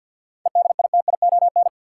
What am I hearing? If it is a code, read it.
Morse code, heard as a keyed tone, about 53 wpm, 700 Hz. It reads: EDITION
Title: Sets of 5 Words - New Word Drill - 50wpm